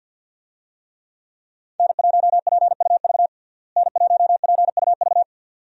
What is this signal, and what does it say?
Morse code, heard as a keyed tone, about 51 wpm, 700 Hz. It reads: N1PUV N1PUV
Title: USA Call Signs - 5 Characters - Repeated 2x - 50wpm